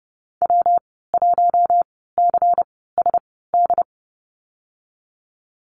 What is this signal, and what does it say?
Morse code, heard as a keyed tone, about 30 wpm, spaced as for 17 wpm, 700 Hz. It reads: W1CSD
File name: Lesson 19g - Letter w - Call Signs 30wpm